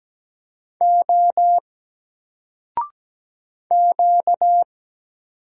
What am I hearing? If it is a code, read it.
Morse code, heard as a keyed tone, about 17 wpm, 700 Hz. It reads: O Q
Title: Single Letter-Number 17wpm